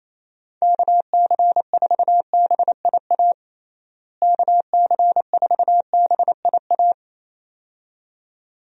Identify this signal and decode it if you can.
Morse code, heard as a keyed tone, about 27 wpm, 700 Hz. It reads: KC4BIA KC4BIA
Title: USA Call Signs - 6 Characters - Repeated 2x - 28wpm